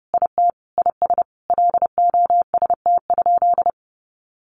Morse code, heard as a keyed tone, about 28 wpm, 700 Hz. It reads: IT IS LOST?
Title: Sentences from Top 1400 Words 30wpm